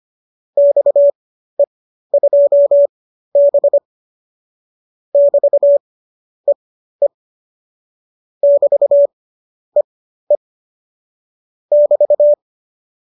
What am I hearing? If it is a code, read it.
Morse code, heard as a keyed tone, about 25 wpm, spaced as for 12 wpm, 570 Hz. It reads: XE2B <BT>EE <BT>EE <BT>